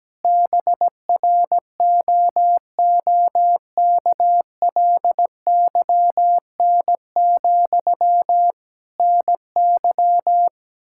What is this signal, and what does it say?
Morse code, heard as a keyed tone, about 17 wpm, 700 Hz. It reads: BROOKLYN, NY
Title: Practice QSO - Exchange 1 - Beginning - 17wpm